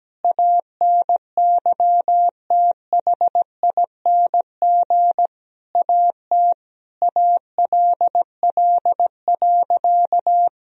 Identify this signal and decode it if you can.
Morse code, heard as a keyed tone, about 17 wpm, 700 Hz. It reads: ANYTHING AT ALL.
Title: Sentences from Top 1000 Words - Review Part 1 - 17wpm